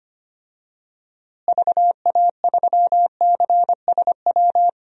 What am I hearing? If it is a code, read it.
Morse code, heard as a keyed tone, about 25 wpm, 700 Hz. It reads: VA3CSW
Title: DXCC-100 Call Signs - 6 Characters - Repeated 2x - 25wpm